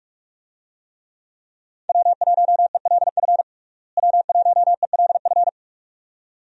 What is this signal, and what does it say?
Morse code, heard as a keyed tone, about 44 wpm, 700 Hz. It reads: W1ELF W1ELF
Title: USA Call Signs - 5 Characters - Repeated 2x - 45wpm